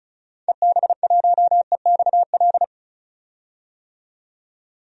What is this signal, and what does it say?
Morse code, heard as a keyed tone, about 34 wpm, 700 Hz. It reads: EB1EXL
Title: DXCC-100 Call Signs - 6 Characters - 35wpm